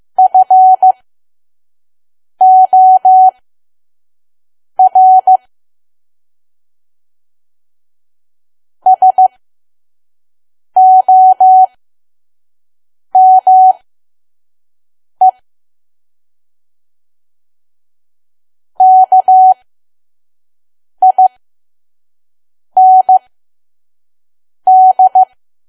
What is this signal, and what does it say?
Morse code, heard as a keyed tone, about 15 wpm, spaced as for 5 wpm, 750 Hz. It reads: FOR SOME KIND
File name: Archive / 5 WPM files / PM